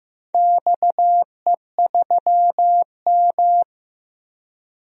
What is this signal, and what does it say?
Morse code, heard as a keyed tone, about 15 wpm, 700 Hz. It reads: XE3M
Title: DXCC-100 Call Signs - 4 Characters - 15wpm